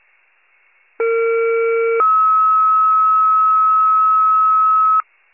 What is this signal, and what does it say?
Two-tone page, heard as a one-second tone, then a three-second tone, about 460 then 1290 Hz. It is heard from 1.0 to 5.0 s.
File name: pager / 21 RESCUE / wcalert